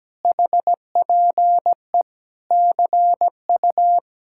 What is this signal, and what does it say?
Morse code, heard as a keyed tone, about 17 wpm, 700 Hz. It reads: HPE CU
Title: Speed-Racing Common QSO Phrases Encore - 17wpm